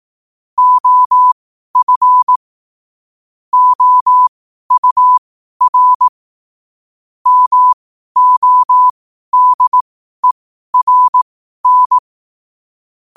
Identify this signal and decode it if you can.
Morse code, heard as a keyed tone, about 18 wpm, spaced as for 12 wpm, 1000 Hz. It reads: OF OUR MODERN
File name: qotdfiles / 12wpm